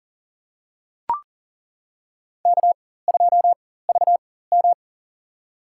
Morse code, heard as a keyed tone, about 40 wpm, spaced as for 19 wpm, 700 Hz. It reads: K2VM
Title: USA Call Signs - 4 Characters - 4x Character Spacing - 40wpm